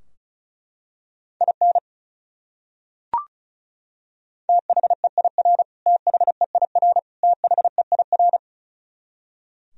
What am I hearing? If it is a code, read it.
Morse code, heard as a keyed tone, about 34 wpm, 700 Hz. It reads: IN THEIR THEIR THEIR
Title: Top 100 Words Repeated 35wpm